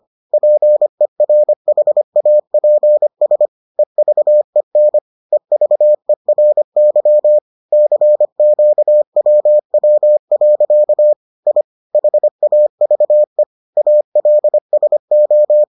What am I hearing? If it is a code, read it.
Morse code, heard as a keyed tone, about 25 wpm, 600 Hz. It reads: PERHAPS EVEN EVERY CQWW. I HAVE ALSO